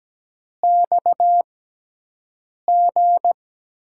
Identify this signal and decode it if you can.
Morse code, heard as a keyed tone, about 17 wpm, spaced as for 6 wpm, 700 Hz. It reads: XG
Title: USA Call Signs - 6x Character Spacing - 17wpm